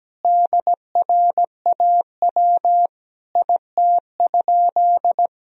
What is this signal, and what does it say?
Morse code, heard as a keyed tone, about 17 wpm, 700 Hz. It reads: DRAW IT?
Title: Sentences from Top 1000 Words - Review Part 1 - 17wpm